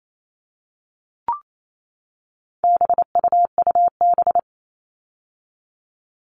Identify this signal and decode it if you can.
Morse code, heard as a keyed tone, about 28 wpm, 700 Hz. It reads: BUUB
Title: All Cut Numbers - 4-Digit 28wpm